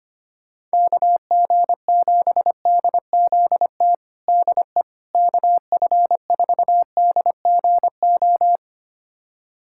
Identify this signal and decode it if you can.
Morse code, heard as a keyed tone, about 25 wpm, 700 Hz. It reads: KG7DZT DE KF4DGO
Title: USA Callsign DE Callsign - 6 Characters - 25wpm